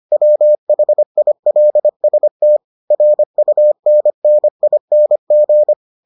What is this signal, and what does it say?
Morse code, heard as a keyed tone, about 25 wpm, 600 Hz. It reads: WHILST RUNNING